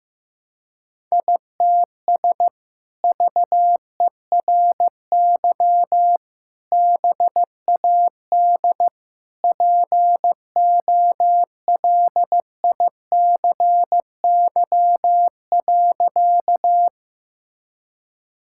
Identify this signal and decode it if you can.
Morse code, heard as a keyed tone, about 15 wpm, 700 Hz. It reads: ITS VERY BAD POLICY.
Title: Sentences from Top 1000 Words - Review Part 5 - 15wpm